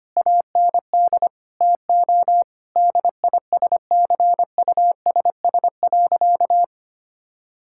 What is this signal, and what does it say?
Morse code, heard as a keyed tone, about 25 wpm, 700 Hz. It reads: AND TO DISCUSS.
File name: practice / 25wpm / Sentences with 2-Word Phrases - 25wpm